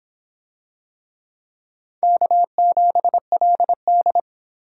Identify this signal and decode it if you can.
Morse code, heard as a keyed tone, about 26 wpm, 700 Hz. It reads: K7LD